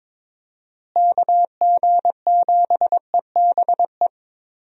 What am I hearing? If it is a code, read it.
Morse code, heard as a keyed tone, about 22 wpm, 700 Hz. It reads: KG7EBE
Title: USA Call Signs - 6 Characters - Repeated 3x - 22wpm